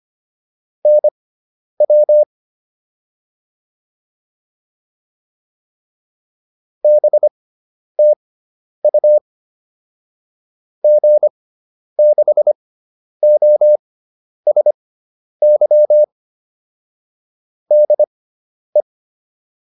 Morse code, heard as a keyed tone, about 25 wpm, spaced as for 10 wpm, 600 Hz. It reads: NW BTU G6OSY DE
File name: QSO409 10